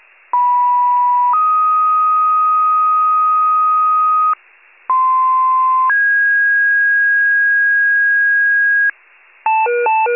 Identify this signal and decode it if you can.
Two-tone page, heard as a one-second tone, then a three-second tone, about 980 then 1250 Hz, about 1040 then 1650 Hz, with noise. It is heard from 0.3 to 8.9 s.
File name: pager / 91 FIRE / wcalert